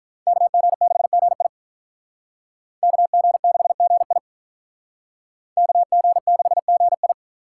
Morse code, heard as a keyed tone, about 45 wpm, 700 Hz. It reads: KG6GI KG6GI KG6GI